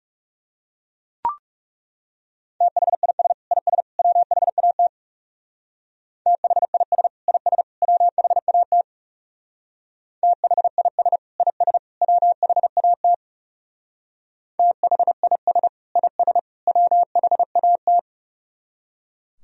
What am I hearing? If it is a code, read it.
Morse code, heard as a keyed tone, about 35 wpm, 700 Hz. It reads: THIS IS WHAT THIS IS WHAT THIS IS WHAT THIS IS WHAT
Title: Speed-Racing - Sets of 3 Words 30wpm